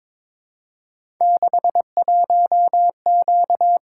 Morse code, heard as a keyed tone, about 22 wpm, 700 Hz. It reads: 61Q
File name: practice / 22wpm / Three Letter-Number Combinations - 22wpm